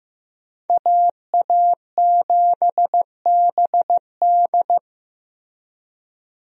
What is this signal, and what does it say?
Morse code, heard as a keyed tone, about 15 wpm, 700 Hz. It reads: AA7BD
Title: USA Call Signs - 5 Characters - Repeated 2x - 15wpm